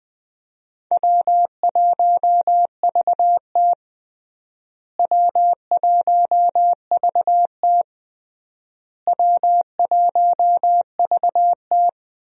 Morse code, heard as a keyed tone, about 20 wpm, 700 Hz. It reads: W1VT W1VT W1VT
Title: USA Call Signs - 4 Characters - Repeated 3x - 20wpm